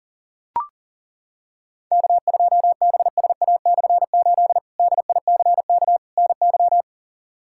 Morse code, heard as a keyed tone, about 40 wpm, 700 Hz. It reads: K2BSA/8 DICK NY
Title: K1USN SST QSO Practice - 40wpm